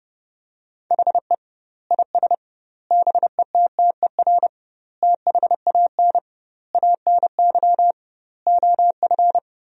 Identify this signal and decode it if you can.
Morse code, heard as a keyed tone, about 30 wpm, 700 Hz. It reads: HE IS BETTER THAN ANY OF
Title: 2x Word-Spacing - Sentences from Top 400 Words - 30wpm